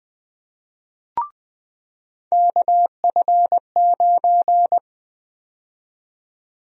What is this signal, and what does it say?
Morse code, heard as a keyed tone, about 20 wpm, 700 Hz. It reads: KF9
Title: USA Call Sign Prefixes - 20wpm